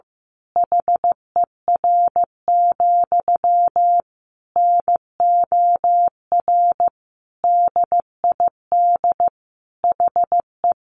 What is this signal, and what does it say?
Morse code, heard as a keyed tone, about 15 wpm, 700 Hz. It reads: HER, NOR DID HE
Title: Sentences from Top 1500 Words 15wpm